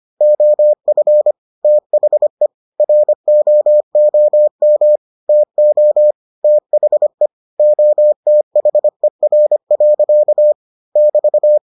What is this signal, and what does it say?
Morse code, heard as a keyed tone, about 25 wpm, 600 Hz. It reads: OF THE ROOM TO THE OTHER. <BT>